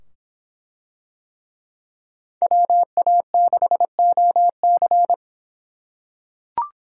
Morse code, heard as a keyed tone, about 23 wpm, 700 Hz. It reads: WA6OC